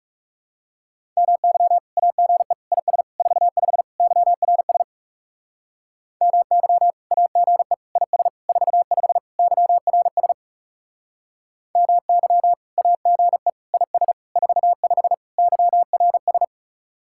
Morse code, heard as a keyed tone, about 36 wpm, 700 Hz. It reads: MY AGE IS 45 YRS MY AGE IS 45 YRS MY AGE IS 45 YRS
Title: Speed-Racing Common QSO Phrases Encore - 30wpm